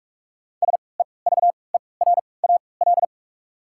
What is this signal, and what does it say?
Morse code, heard as a keyed tone, about 46 wpm, spaced as for 26 wpm, 700 Hz. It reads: SEVERAL
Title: 3x Character Spacing - Top 500 Words - 45wpm